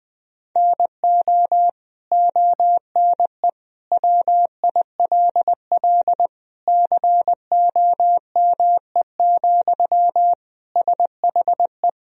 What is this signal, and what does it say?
Morse code, heard as a keyed tone, about 20 wpm, 700 Hz. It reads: NO ONE WILL COME, SHE
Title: Sentences from Top 100 Words and Abbreviations 20wpm